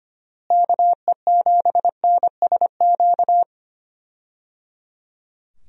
Morse code, heard as a keyed tone, about 25 wpm, 700 Hz. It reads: KE7NSQ
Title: USA Call Signs - 6 Characters - Repeated 3x - 25wpm